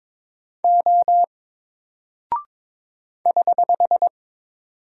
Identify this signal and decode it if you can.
Morse code, heard as a keyed tone, about 22 wpm, 700 Hz. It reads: O <HH>
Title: Single Character (Letter:Number:Punctuation:Prosign) 22wpm